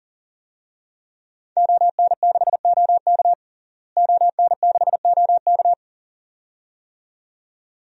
Morse code, heard as a keyed tone, about 40 wpm, 700 Hz. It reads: ON6OK ON6OK
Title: DXCC-100 Call Signs - 5 Characters - Repeated 2x - 40wpm